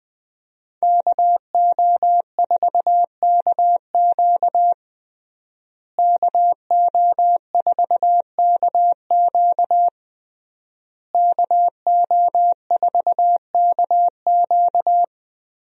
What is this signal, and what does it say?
Morse code, heard as a keyed tone, about 20 wpm, 700 Hz. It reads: KO4KQ KO4KQ KO4KQ
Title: USA Call Signs - 5 Characters - Repeated 3x - 20wpm